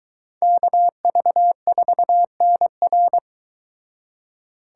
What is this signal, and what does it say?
Morse code, heard as a keyed tone, about 23 wpm, 700 Hz. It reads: KV4NR